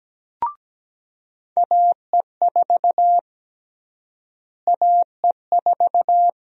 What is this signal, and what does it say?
Morse code, heard as a keyed tone, about 17 wpm, 700 Hz. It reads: AE4 AE4
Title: USA Call Sign Prefixes - Repeated 2x - 17wpm